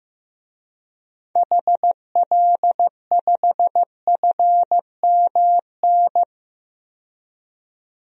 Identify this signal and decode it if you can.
Morse code, heard as a keyed tone, about 15 wpm, 700 Hz. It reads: HL5FMN
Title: DXCC-100 Call Signs - 6 Characters - 15wpm